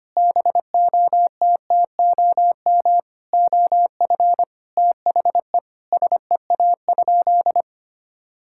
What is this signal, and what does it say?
Morse code, heard as a keyed tone, about 25 wpm, 700 Hz. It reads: BOTTOM OF THE SEA?
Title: Sentences from Top 1000 Words - Review Part 6 - 25wpm